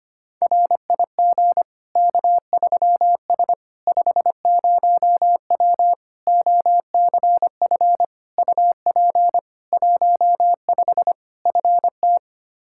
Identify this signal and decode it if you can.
Morse code, heard as a keyed tone, about 25 wpm, 700 Hz. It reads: RIG K3S 50W OCF UP 15 FT
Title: K1USN SST QSO Practice 25wpm